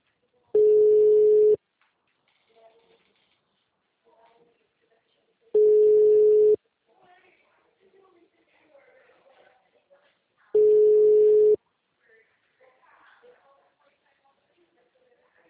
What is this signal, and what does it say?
Ringback tone (Europe, 425 Hz), with speech.
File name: out- -2018- - - .51